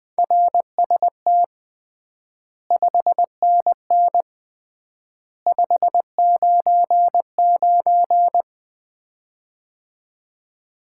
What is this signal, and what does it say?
Morse code, heard as a keyed tone, about 20 wpm, 700 Hz. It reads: RST 5NN 599
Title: Common QSO Phrases - 3x Word-Spacing - 20wpm